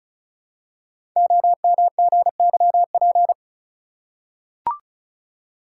Morse code, heard as a keyed tone, about 36 wpm, 690 Hz. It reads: OMGYP
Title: Code Groups - 5 Letters - 35wpm